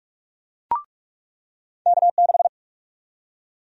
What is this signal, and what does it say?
Morse code, heard as a keyed tone, about 46 wpm, 700 Hz. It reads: K6
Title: USA Call Sign Prefixes - 45wpm